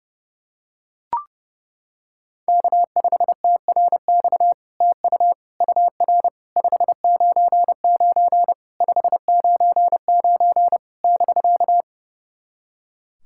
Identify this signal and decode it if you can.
Morse code, heard as a keyed tone, about 32 wpm, 700 Hz. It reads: K5TRX TU UR 599 599 <BK>
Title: SOTA QSO Practice 30wpm